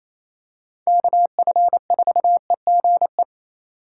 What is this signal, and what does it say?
Morse code, heard as a keyed tone, about 27 wpm, 700 Hz. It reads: KF4EGE
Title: USA Call Signs - 6 Characters - 3x Character Spacing - 28wpm